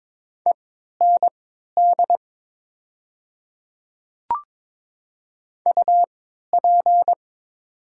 Morse code, heard as a keyed tone, about 22 wpm, spaced as for 13 wpm, 700 Hz. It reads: END UP
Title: 3x Character Spacing - Top 200 Words - 22wpm